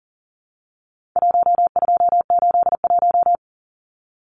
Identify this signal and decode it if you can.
Morse code, heard as a keyed tone, about 40 wpm, 700 Hz. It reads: 1281
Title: Numbers - 4-Digit 40wpm